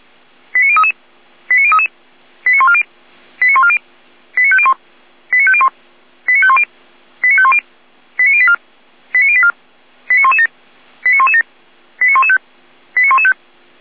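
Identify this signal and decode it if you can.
Five-tone selective call (ZVEI), heard as quick runs of five tones, tones of about 70 ms. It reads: 89033 89033 89140 89140 89501 89501 89510 89510 89084 89084 89117 89117 89115 89115
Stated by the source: Five-tone ZVEI selective call received by SDR.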